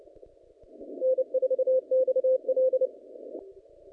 Morse code, heard as a keyed tone, about 30 wpm, 520 Hz. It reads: N4XL
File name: contests / naqp-cw-k5tr-op-k5pi- -0300